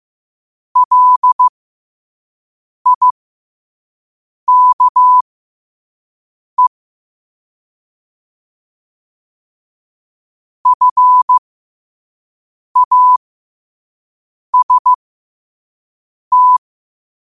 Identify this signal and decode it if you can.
Morse code, heard as a keyed tone, about 15 wpm, spaced as for 5 wpm, 1000 Hz. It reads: LIKE FAST
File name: qotdfiles / 5wpm